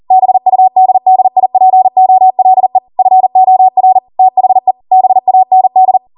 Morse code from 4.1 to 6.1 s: THE BAND